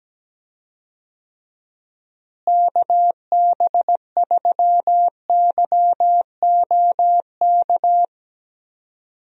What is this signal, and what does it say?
Morse code, heard as a keyed tone, about 17 wpm, 700 Hz. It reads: KB3YOK
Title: USA Call Signs - 6 Characters - Repeated 3x - 17wpm